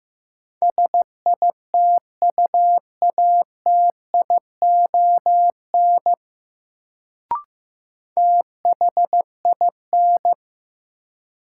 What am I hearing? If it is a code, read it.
Morse code, heard as a keyed tone, about 16 wpm, 700 Hz. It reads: SITUATION THIN
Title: Sentences from Top 1200 Words - New Word Drill - 15wpm